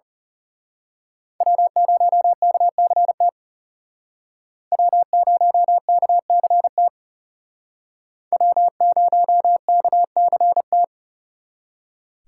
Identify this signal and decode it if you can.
Morse code, heard as a keyed tone, about 34 wpm, 700 Hz. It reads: W0KCT W0KCT W0KCT
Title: Speed Racing - USA 1x3 Call Signs 30wpm